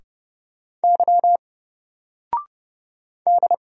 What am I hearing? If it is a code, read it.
Morse code, heard as a keyed tone, about 30 wpm, 700 Hz. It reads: Y D
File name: Single-Letter-Number-30wpm